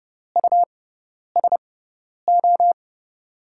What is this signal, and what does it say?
Morse code, heard as a keyed tone, about 31 wpm, spaced as for 10 wpm, 700 Hz. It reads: USO
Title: USA Call Signs - 5 Characters - 6x Character Spacing - 30wpm